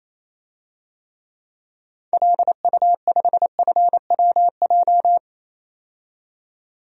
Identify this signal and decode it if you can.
Morse code, heard as a keyed tone, about 28 wpm, 700 Hz. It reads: LU5FWJ
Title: DXCC-100 Call Signs - 6 Characters - 28wpm